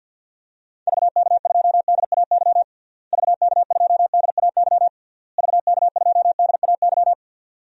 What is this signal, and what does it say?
Morse code, heard as a keyed tone, about 51 wpm, 700 Hz. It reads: VK2DAY VK2DAY VK2DAY
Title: DXCC-100 Call Signs - 6 Characters - Repeated 3x - 50wpm